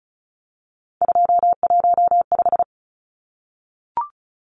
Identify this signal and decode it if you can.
Morse code, heard as a keyed tone, about 34 wpm, 700 Hz. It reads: 215
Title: Numbers - 3-Digit 35wpm